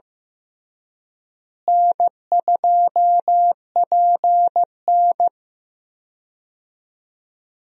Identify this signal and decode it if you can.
Morse code, heard as a keyed tone, about 14 wpm, 700 Hz. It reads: N2PN
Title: USA Call Signs - 4 Characters - Repeated 3x - 15wpm